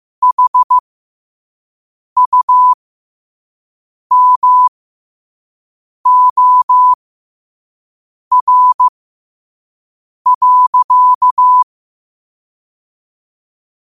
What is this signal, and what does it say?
Morse code, heard as a keyed tone, about 15 wpm, spaced as for 5 wpm, 1000 Hz. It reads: HUMOR.
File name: qotdfiles / 5wpm